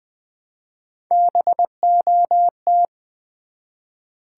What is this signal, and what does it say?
Morse code, heard as a keyed tone, about 20 wpm, 700 Hz. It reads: BOT
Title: Build A Word - 1 Letter At A Time - 4-Letter Words - 20wpm